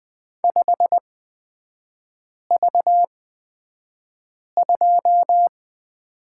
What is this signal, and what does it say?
Morse code, heard as a keyed tone, about 20 wpm, spaced as for 5 wpm, 690 Hz. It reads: 5V2
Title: 229 Lesson 37e - Letter x - Three Characters 20wpm